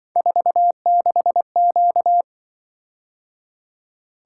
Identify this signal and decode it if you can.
Morse code, heard as a keyed tone, about 25 wpm, 700 Hz. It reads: 46Q
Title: Speed-Racing - Three Letter-Number Combinations - 20wpm